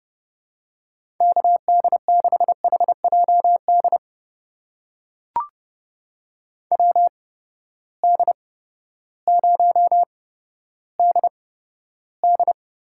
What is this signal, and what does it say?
Morse code, heard as a keyed tone, about 30 wpm, spaced as for 14 wpm, 700 Hz. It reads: KD6HJD WD0DD
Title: USA Call Signs - 6 Characters - 8x Character Spacing - 30wpm